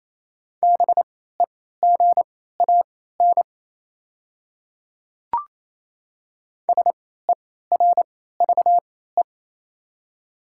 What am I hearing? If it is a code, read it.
Morse code, heard as a keyed tone, about 28 wpm, spaced as for 16 wpm, 690 Hz. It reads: BEGAN SERVE
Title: 3x Character Spacing - Top 500 Words - 28wpm